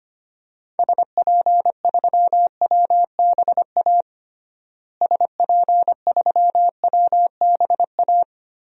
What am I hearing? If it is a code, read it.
Morse code, heard as a keyed tone, about 25 wpm, 700 Hz. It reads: SP3WBA SP3WBA
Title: DXCC-100 Call Signs - 6 Characters - Repeated 2x - 25wpm